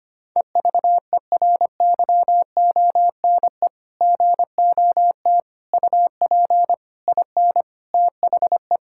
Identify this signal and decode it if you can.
Morse code, heard as a keyed tone, about 25 wpm, 700 Hz. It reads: EVERYONE GOT UP IN THE
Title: Sentences from Top 1000 Words - Review Part 5 - 25wpm